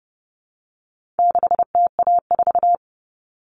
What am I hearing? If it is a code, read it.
Morse code, heard as a keyed tone, about 28 wpm, 700 Hz. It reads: 6TA4
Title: All Cut Numbers - 4-Digit 30wpm